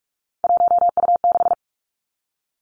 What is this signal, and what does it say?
Morse code, heard as a keyed tone, about 46 wpm, 700 Hz. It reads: 1U6
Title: Three Challenging Letter-Number Combinations 45wpm